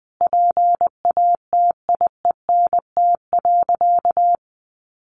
Morse code, heard as a keyed tone, about 20 wpm, 700 Hz. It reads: PATIENT.
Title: Additional Easy Phrases 20wpm